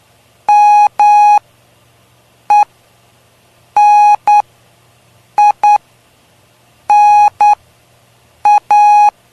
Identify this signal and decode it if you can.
Morse code, heard as a keyed tone, about 9 wpm, spaced as for 6 wpm, 830 Hz. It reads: MENINA
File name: cw / py2lcd / 03-PY2LCD-Li��o 3